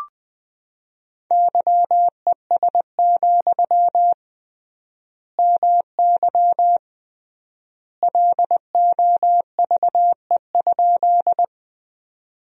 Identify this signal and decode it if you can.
Morse code, heard as a keyed tone, about 20 wpm, 700 Hz. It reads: YES, MY LOVE?
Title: 3x Word-Spacing - Sentences from Top 500 Words - 20wpm